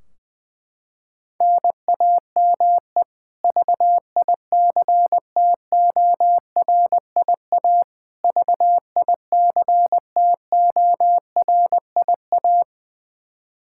Morse code, heard as a keyed tone, about 20 wpm, 700 Hz. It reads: NAME VICTORIA VICTORIA
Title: K1USN SST QSO Practice 20wpm